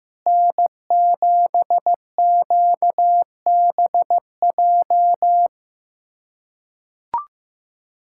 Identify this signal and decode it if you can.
Morse code, heard as a keyed tone, about 15 wpm, 700 Hz. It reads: N7QBJ
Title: USA Callsign DE Callsign - 5 Characters - 15wpm